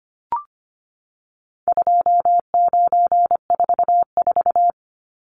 Morse code, heard as a keyed tone, about 25 wpm, 700 Hz. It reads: 2944
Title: Numbers - 4-Digit 25wpm